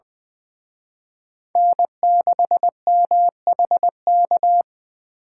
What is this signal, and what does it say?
Morse code, heard as a keyed tone, about 23 wpm, 700 Hz. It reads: N6MHK
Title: Speed Racing - USA 1x3 Call Signs 20wpm